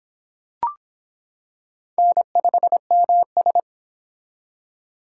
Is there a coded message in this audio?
Yes: Morse code from 2.0 to 3.6 s, heard as a keyed tone, about 25 wpm, 700 Hz. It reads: N5MS